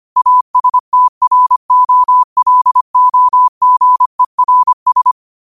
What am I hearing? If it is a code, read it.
Morse code, heard as a keyed tone, about 25 wpm, 1000 Hz. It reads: ASTROLOGERS
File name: qotdfiles / 25wpm